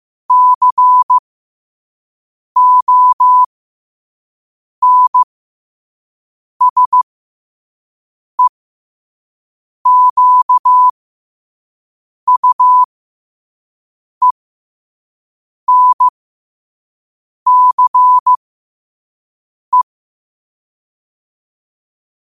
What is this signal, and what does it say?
Morse code, heard as a keyed tone, about 15 wpm, spaced as for 5 wpm, 1000 Hz. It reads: CONSEQUENCE
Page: New quotes every day in morse code at 5 Words per minute.